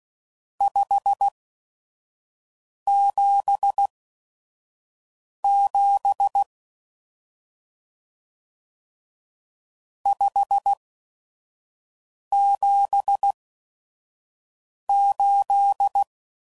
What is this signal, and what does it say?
Morse code, heard as a keyed tone, about 16 wpm, spaced as for 5 wpm, 800 Hz. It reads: 577 578
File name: CW Code Groups @5WPM / RST's / Various RST No's